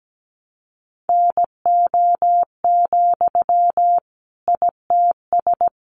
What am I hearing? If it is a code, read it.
Morse code, heard as a keyed tone, about 17 wpm, 700 Hz. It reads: NO, ITS
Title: Sentences from Top 600 Words 17wpm